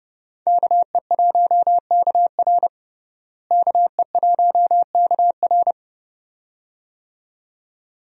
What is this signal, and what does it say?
Morse code, heard as a keyed tone, about 30 wpm, 700 Hz. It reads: KE1KR KE1KR
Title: USA Call Signs - 5 Characters - Repeated 2x - 30wpm